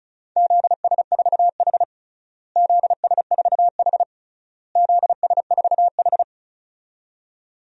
Morse code, heard as a keyed tone, about 34 wpm, 700 Hz. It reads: ZS4H ZS4H ZS4H
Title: DXCC-100 Call Signs - 4 Characters - Repeated 3x - 35wpm